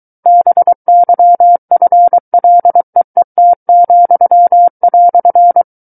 Morse code, heard as a keyed tone, about 23 wpm, 700 Hz. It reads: BYFLEET,"
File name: Book One / 14-Chapter 14 - 23 WPM